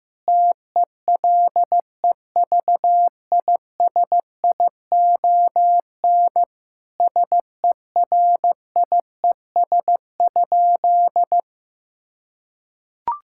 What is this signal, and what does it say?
Morse code, heard as a keyed tone, about 16 wpm, 700 Hz. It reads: TELEVISION SERIES?
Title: Sentences from Top 1000 Words - Review Part 5 - 15wpm